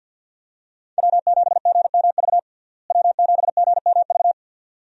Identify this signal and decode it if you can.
Morse code, heard as a keyed tone, about 47 wpm, 700 Hz. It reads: W7GMV W7GMV
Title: USA Call Signs - 5 Characters - Repeated 2x - 50wpm